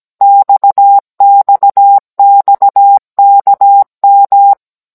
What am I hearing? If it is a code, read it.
Morse code, heard as a keyed tone, about 17 wpm, 800 Hz. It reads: XXXKM